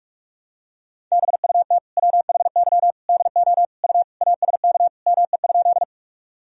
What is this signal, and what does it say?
Morse code, heard as a keyed tone, about 46 wpm, 700 Hz. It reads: BUT WHY DO U ASK ME?
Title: Sentences from Top 200 Words and Abbreviations 45wpm